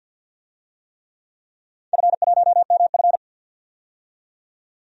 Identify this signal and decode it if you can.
Morse code, heard as a keyed tone, about 46 wpm, 700 Hz. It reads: F1MF
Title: DXCC-100 Call Signs - 4 Characters - 50wpm